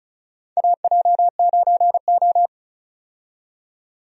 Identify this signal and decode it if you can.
Morse code, heard as a keyed tone, about 36 wpm, 700 Hz. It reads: AJ9O
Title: USA Call Signs - 4 Characters - 35wpm